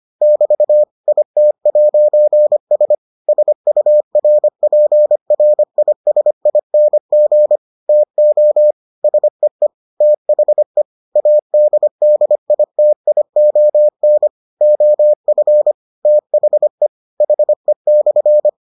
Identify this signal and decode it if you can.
Morse code, heard as a keyed tone, about 25 wpm, 600 Hz. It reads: <BT> IT'S SURPRISING TO SEE THE ADDITION OF THE HE/